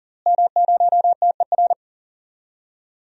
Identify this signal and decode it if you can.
Morse code, heard as a keyed tone, about 40 wpm, 700 Hz. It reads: M0TER
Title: DXCC-100 Call Signs - 5 Characters - Repeated 2x - 40wpm